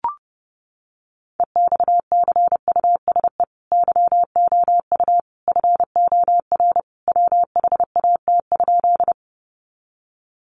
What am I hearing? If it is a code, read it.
Morse code, heard as a keyed tone, about 30 wpm, 700 Hz. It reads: EXCUSE YOU FOR WHAT?
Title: Sentences from Top 700 Words 30wpm